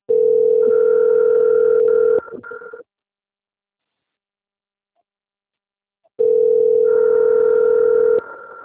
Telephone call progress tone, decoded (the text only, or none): ringback tone